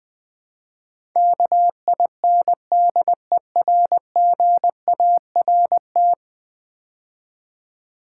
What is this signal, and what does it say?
Morse code, heard as a keyed tone, about 19 wpm, 700 Hz. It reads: KINDERGART
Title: Build A Word - 5 Letters At A Time - 12-Letter Words - 20wpm